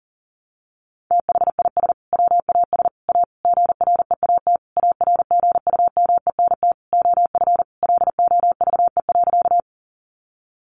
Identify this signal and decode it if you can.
Morse code, heard as a keyed tone, about 40 wpm, 700 Hz. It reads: THIS WAS A GREAT ARGUMENT OF LOVE.
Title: Sentences from Top 1400 Words 40wpm